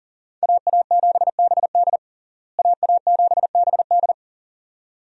Morse code, heard as a keyed tone, about 40 wpm, 700 Hz. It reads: AA7BD AA7BD
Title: USA Call Signs - 5 Characters - Repeated 2x - 40wpm